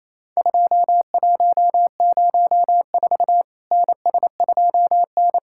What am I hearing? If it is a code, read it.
Morse code, heard as a keyed tone, about 28 wpm, 700 Hz. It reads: 2104 NS2N
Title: Worked All Europe DX Contest - QTC Exchange - 28wpm